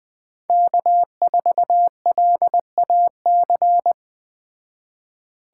Morse code, heard as a keyed tone, about 20 wpm, 700 Hz. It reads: K4LAC